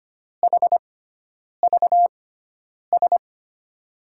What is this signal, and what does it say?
Morse code, heard as a keyed tone, about 25 wpm, spaced as for 9 wpm, 700 Hz. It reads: HVS
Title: USA Call Signs - 6x Character Spacing - 25wpm